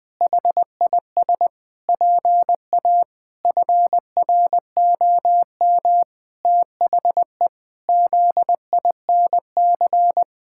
Morse code, heard as a keyed tone, about 20 wpm, 700 Hz. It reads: HIS PA FROM THE ZINC
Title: Pangram Challenge v2 - 20wpm